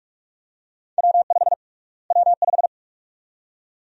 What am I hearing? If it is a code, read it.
Morse code, heard as a keyed tone, about 44 wpm, 700 Hz. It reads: W5 W5
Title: USA Call Sign Prefixes - Repeated 2x - 45wpm